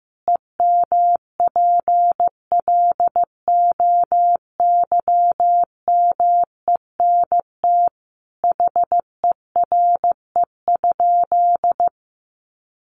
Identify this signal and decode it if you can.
Morse code, heard as a keyed tone, about 15 wpm, 700 Hz. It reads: EMPLOYMENT HERE?
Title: Sentences from Top 1500 Words 15wpm